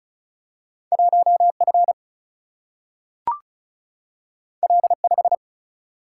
Morse code, heard as a keyed tone, about 36 wpm, 700 Hz. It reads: 1 F L 5
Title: Single-Letter Single-Number 35wpm